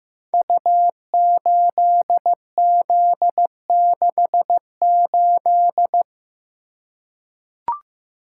Morse code, heard as a keyed tone, about 15 wpm, 700 Hz. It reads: U8Z68
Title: Code Groups - 5 Letter:Numbers - 15wpm